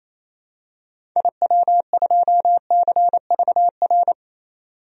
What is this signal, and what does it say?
Morse code, heard as a keyed tone, about 28 wpm, 700 Hz. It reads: IW2CVR
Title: DXCC-100 Call Signs - 6 Characters - 28wpm